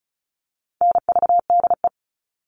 Morse code, heard as a keyed tone, about 34 wpm, 700 Hz. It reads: NVDE
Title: All Cut Numbers - 4-Digit 35wpm